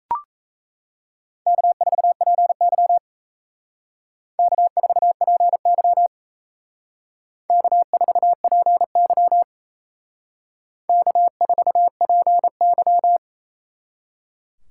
Morse code, heard as a keyed tone, about 35 wpm, 700 Hz. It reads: K4PY K4PY K4PY K4PY
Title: Speed Racing - USA 1x2 Call Signs 28wpm